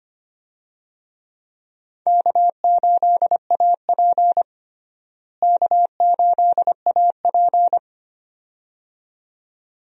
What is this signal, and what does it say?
Morse code, heard as a keyed tone, about 25 wpm, 700 Hz. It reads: K8AP K8AP
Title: USA Call Signs - 4 Characters - Repeated 2x - 25wpm